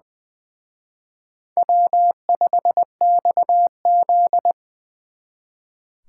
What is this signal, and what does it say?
Morse code, heard as a keyed tone, about 24 wpm, 700 Hz. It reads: W5XZ
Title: Speed Racing - USA 1x2 Call Signs 20wpm